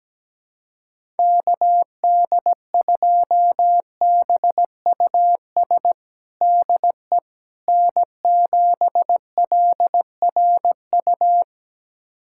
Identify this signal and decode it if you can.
Morse code, heard as a keyed tone, about 17 wpm, 700 Hz. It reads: KD2BUS DE N7LRU
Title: USA Callsign DE Callsign 17wpm